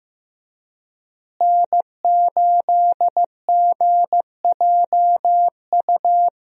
Morse code, heard as a keyed tone, about 15 wpm, 700 Hz. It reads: N8GJU
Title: USA Call Signs - 5 Characters - Repeated 2x - 15wpm